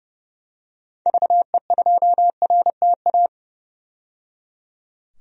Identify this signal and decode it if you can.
Morse code, heard as a keyed tone, about 30 wpm, 700 Hz. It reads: VE2RTA
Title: DXCC-100 Call Signs - 6 Characters - 30wpm